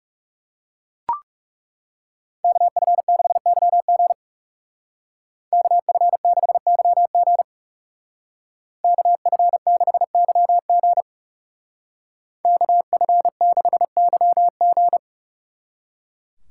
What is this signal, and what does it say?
Morse code, heard as a keyed tone, about 39 wpm, 700 Hz. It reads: KF6YG KF6YG KF6YG KF6YG